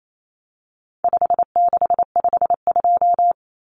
Morse code, heard as a keyed tone, about 28 wpm, 700 Hz. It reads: 5652
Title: Common Cut Numbers - 4-Digit 28wpm